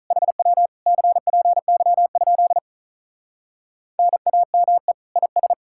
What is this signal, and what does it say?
Morse code, heard as a keyed tone, about 46 wpm, 700 Hz. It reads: HW CPY? NAME IS